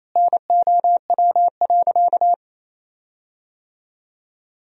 Morse code, heard as a keyed tone, about 28 wpm, 700 Hz. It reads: NOW.
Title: 4x Word-Spacing - Sentences from Top 500 Words - 28wpm